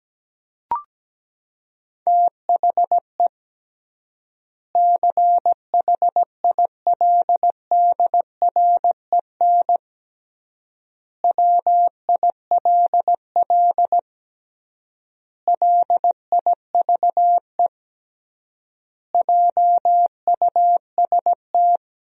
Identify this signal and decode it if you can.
Morse code, heard as a keyed tone, about 17 wpm, 700 Hz. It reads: THE CHILDREN WILL LIVE JUST
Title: 3x Word-Spacing - Sentences from Top 300 Words - 17wpm